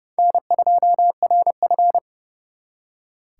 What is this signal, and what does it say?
Morse code, heard as a keyed tone, about 36 wpm, 700 Hz. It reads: N2RF
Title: Speed Racing - USA 1x2 Call Signs 30wpm